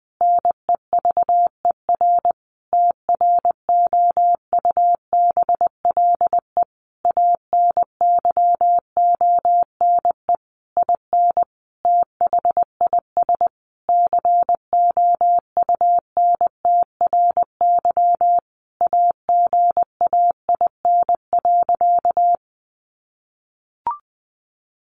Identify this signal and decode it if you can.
Morse code, heard as a keyed tone, about 20 wpm, 700 Hz. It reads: NEVER TROUBLE ANYONE IN THIS COUNTRY AGAIN.
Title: Sentences from Top 1200 Words 20wpm